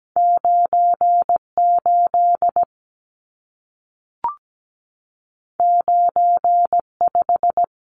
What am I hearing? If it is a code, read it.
Morse code, heard as a keyed tone, about 17 wpm, 700 Hz. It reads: 98 95
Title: Numbers - 2-Digit 17wpm